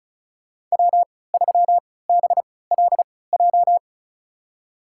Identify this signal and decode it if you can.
Morse code, heard as a keyed tone, about 34 wpm, spaced as for 20 wpm, 700 Hz. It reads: W3BLJ
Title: USA Call Signs - 5 Characters - 3x Character Spacing - 35wpm